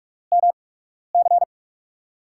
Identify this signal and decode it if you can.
Morse code, heard as a keyed tone, about 47 wpm, spaced as for 12 wpm, 700 Hz. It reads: MC
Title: USA Call Signs - 8x Character Spacing - 45wpm